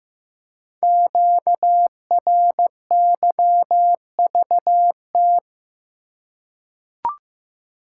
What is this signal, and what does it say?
Morse code, heard as a keyed tone, about 15 wpm, 700 Hz. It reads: QRYVT
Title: Code Groups - 5 Letter:Numbers - 15wpm